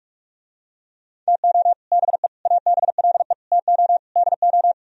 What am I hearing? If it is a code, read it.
Morse code, heard as a keyed tone, about 48 wpm, 700 Hz. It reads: TO BE ABLE TO DO